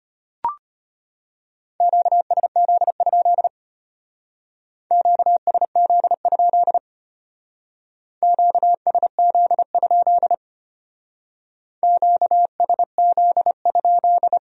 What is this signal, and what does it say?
Morse code, heard as a keyed tone, about 33 wpm, 700 Hz. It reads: QSZ? QSZ? QSZ? QSZ?